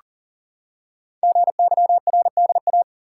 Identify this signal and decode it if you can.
Morse code, heard as a keyed tone, about 40 wpm, 700 Hz. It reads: GYRDA
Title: Code Groups - 5 Letters - 40wpm